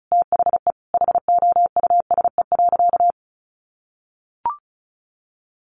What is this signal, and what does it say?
Morse code, heard as a keyed tone, about 34 wpm, 700 Hz. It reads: THE HOUSE.
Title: Sentences from Top 1300 Words 35wpm